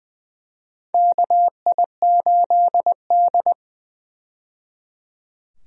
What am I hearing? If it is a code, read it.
Morse code, heard as a keyed tone, about 20 wpm, 700 Hz. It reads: KI8D
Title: USA Call Signs - 4 Characters - 20wpm